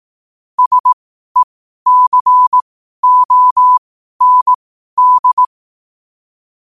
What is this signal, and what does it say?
Morse code, heard as a keyed tone, about 18 wpm, spaced as for 12 wpm, 1000 Hz. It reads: SECOND
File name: qotdfiles / 12wpm